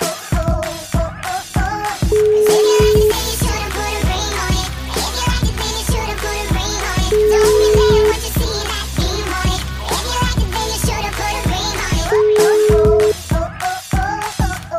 Ringback tone (Europe, 425 Hz), with noise.